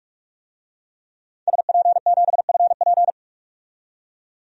Morse code, heard as a keyed tone, about 46 wpm, 700 Hz. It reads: SP7FP
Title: DXCC-100 Call Signs - 5 Characters - 45wpm